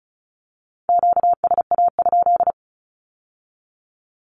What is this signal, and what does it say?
Morse code, heard as a keyed tone, about 36 wpm, 700 Hz. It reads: QSA?
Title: Q-Codes with Spoken Meaning - 35wpm